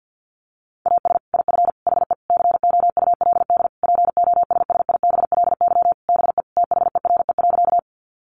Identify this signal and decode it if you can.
Morse code, heard as a keyed tone, about 50 wpm, 700 Hz. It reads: AS IF HE COULD POSSIBLY BE THERE.
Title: Sentences from Top 1300 Words 50wpm